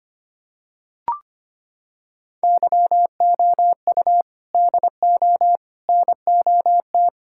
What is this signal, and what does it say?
Morse code, heard as a keyed tone, about 25 wpm, 700 Hz. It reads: YOU DO NOT
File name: practice / 25wpm / Sentences with 4-Word Phrases - 25wpm